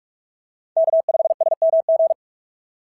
Morse code, heard as a keyed tone, about 44 wpm, 680 Hz. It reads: K5SMG
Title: SOTA QSO Practice 45wpm